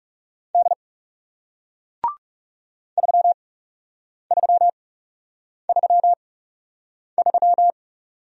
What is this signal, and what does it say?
Morse code, heard as a keyed tone, about 40 wpm, spaced as for 8 wpm, 700 Hz. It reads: D 3333